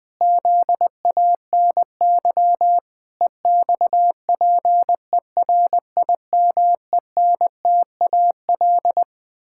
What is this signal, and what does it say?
Morse code, heard as a keyed tone, about 20 wpm, 700 Hz. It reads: ZANY EXPERIMENTAL
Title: Pangram Challenge v2 - 20wpm